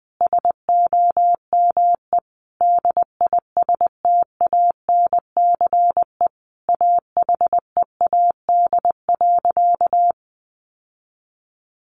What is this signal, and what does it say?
Morse code, heard as a keyed tone, about 20 wpm, 700 Hz. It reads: SOME DISTANCE AHEAD.
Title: Sentences from Top 1400 Words 20wpm